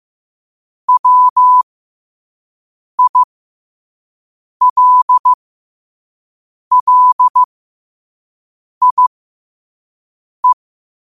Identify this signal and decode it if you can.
Morse code, heard as a keyed tone, about 15 wpm, spaced as for 5 wpm, 1000 Hz. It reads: WILLIE
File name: qotdfiles / 5wpm